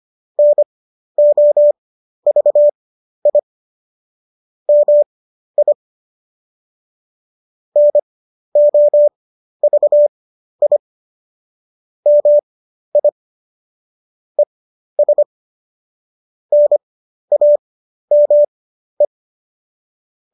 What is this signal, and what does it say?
Morse code, heard as a keyed tone, about 25 wpm, spaced as for 12 wpm, 590 Hz. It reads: NOVI MI NOVI MI ES NAME